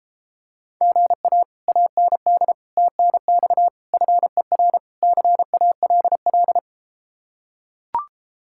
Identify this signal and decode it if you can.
Morse code, heard as a keyed tone, about 33 wpm, 700 Hz. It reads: GA AND TNX FER CALL